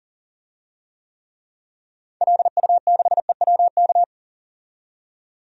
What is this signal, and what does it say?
Morse code, heard as a keyed tone, about 40 wpm, 700 Hz. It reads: LU6EWK
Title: DXCC-100 Call Signs - 6 Characters - 40wpm